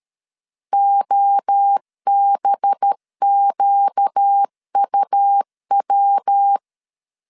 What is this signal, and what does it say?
Morse code, heard as a keyed tone, about 13 wpm, 800 Hz. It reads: OBQUW
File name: CW / DET / 1 6 1500C S A 12wpm 3440